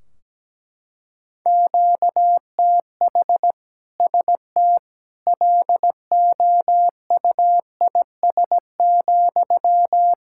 Morse code, heard as a keyed tone, about 17 wpm, 700 Hz. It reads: QTH ST LOUIS,
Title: Practice QSO - Exchange 1 - Beginning - 17wpm